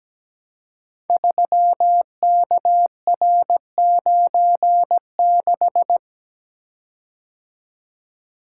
Morse code, heard as a keyed tone, about 17 wpm, 700 Hz. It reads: 3KR96
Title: Code Groups - 5 Letter:Numbers - 17wpm